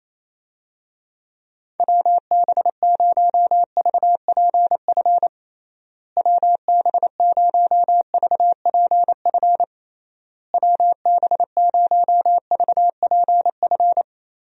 Morse code, heard as a keyed tone, about 28 wpm, 700 Hz. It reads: WB0VPF WB0VPF WB0VPF
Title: USA Call Signs - 6 Characters - Repeated 3x - 28wpm